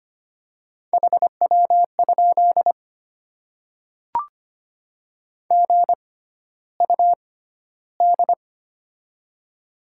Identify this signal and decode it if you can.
Morse code, heard as a keyed tone, about 25 wpm, spaced as for 13 wpm, 700 Hz. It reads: HW? GUD
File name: Top 20 QSO Elements - 6x Character Spacing - 25wpm